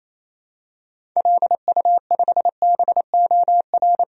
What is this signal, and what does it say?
Morse code, heard as a keyed tone, about 28 wpm, 700 Hz. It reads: LU5BOR
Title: DXCC-100 Call Signs - 6 Characters - 28wpm